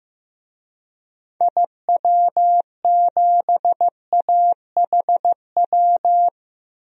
Morse code, heard as a keyed tone, about 15 wpm, 700 Hz. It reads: IW7AHW
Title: DXCC-100 Call Signs - 6 Characters - 15wpm